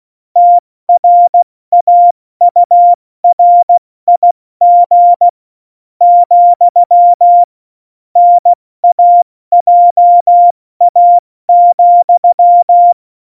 Morse code, heard as a keyed tone, about 16 wpm, spaced as for 13 wpm, 700 Hz. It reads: TRAURIG, NAJA,